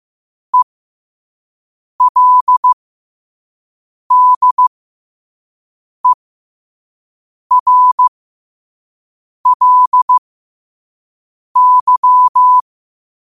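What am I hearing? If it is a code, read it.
Morse code, heard as a keyed tone, about 15 wpm, spaced as for 5 wpm, 1000 Hz. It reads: ELDERLY
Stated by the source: Morse at 5 wpm.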